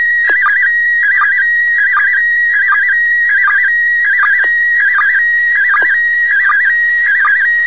SSTV (slow-scan television), PD 180.